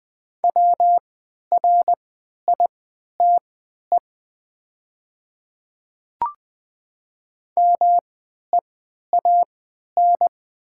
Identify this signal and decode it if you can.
Morse code, heard as a keyed tone, about 20 wpm, spaced as for 11 wpm, 700 Hz. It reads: WRITE MEAN
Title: 3x Character Spacing - Top 200 Words - 20wpm